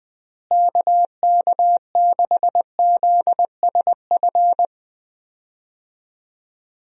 Morse code, heard as a keyed tone, about 20 wpm, 700 Hz. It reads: KK6ZSF
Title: USA Call Signs - 6 Characters - Repeated 2x - 20wpm